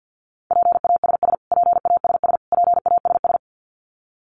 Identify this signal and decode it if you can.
Morse code, heard as a keyed tone, about 50 wpm, 700 Hz. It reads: PASS PASS PASS
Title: Top 500 Words Repeated 50wpm